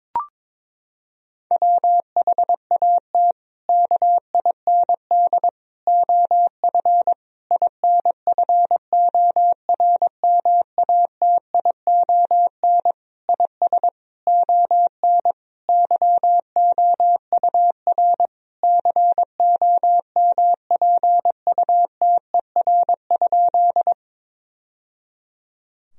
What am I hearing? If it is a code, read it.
Morse code, heard as a keyed tone, about 22 wpm, 700 Hz. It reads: WHAT KIND OF INFORMATION IS ON YOUR COMPUTER?
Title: Sentences from Top 1000 Words - Review Part 3 - 22wpm